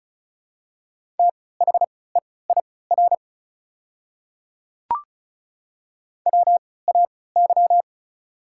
Morse code, heard as a keyed tone, about 34 wpm, spaced as for 20 wpm, 690 Hz. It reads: THEIR WAY
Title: 3x Character Spacing - Top 100 Words - 35wpm